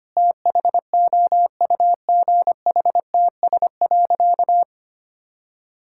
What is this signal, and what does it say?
Morse code, heard as a keyed tone, about 25 wpm, 700 Hz. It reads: THOUGHTS.
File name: practice / 25wpm / Difficult Sentences with Hints - 25wpm